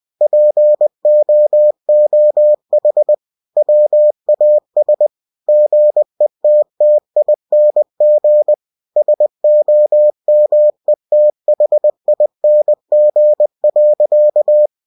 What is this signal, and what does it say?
Morse code, heard as a keyed tone, about 20 wpm, 600 Hz. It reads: POOH WAS GETTING SOMETHING.
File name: winnie-20wpm